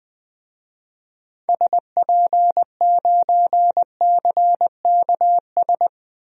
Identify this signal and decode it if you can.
Morse code, heard as a keyed tone, about 20 wpm, 700 Hz. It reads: SP9CKS
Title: DXCC-100 Call Signs - 6 Characters - 20wpm